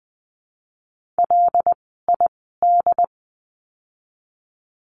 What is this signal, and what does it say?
Morse code, heard as a keyed tone, about 19 wpm, spaced as for 14 wpm, 700 Hz. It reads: LID
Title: 075 Lesson 13f - Letter L - Words 20wpm